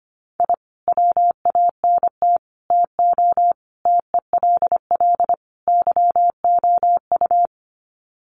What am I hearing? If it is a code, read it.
Morse code, heard as a keyed tone, about 25 wpm, 700 Hz. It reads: I WANT TO TELL YOU
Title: Sets of 5 Words Encore 25wpm